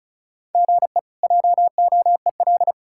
Morse code, heard as a keyed tone, about 36 wpm, 700 Hz. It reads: GE JOEL
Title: K1USN SST QSO Practice - 35wpm